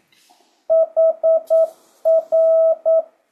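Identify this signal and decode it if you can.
Morse code, heard as a keyed tone, about 9 wpm, 650 Hz. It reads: HR
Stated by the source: Morse code speed-up practice at 9 WPM, from JH8CBH.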